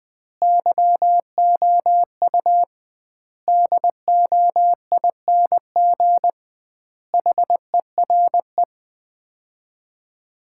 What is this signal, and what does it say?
Morse code, heard as a keyed tone, about 20 wpm, 700 Hz. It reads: YOU DOING HERE
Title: 2x Word-Spacing - Sets of 5 Words - 20wpm